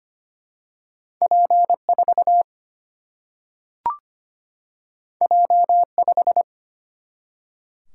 Morse code, heard as a keyed tone, about 25 wpm, 700 Hz. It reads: P4 J5
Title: Single-Letter Single-Number 25wpm